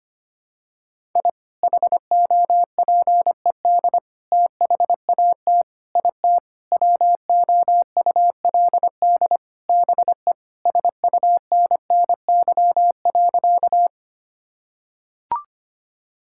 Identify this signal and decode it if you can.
Morse code, heard as a keyed tone, about 25 wpm, 700 Hz. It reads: I HOPED THAT IT WOULD BE SUNNY.
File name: practice / 25wpm / Sentences with 4-Word Phrases - 25wpm